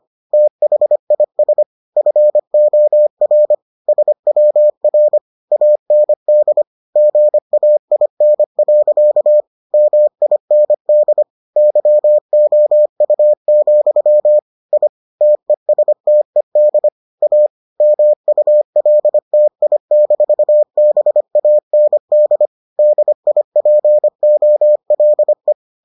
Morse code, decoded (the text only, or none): THIS FOR SWR AND GAIN. MIND YOU, I TESTED A MULTI-BAND DIPOLE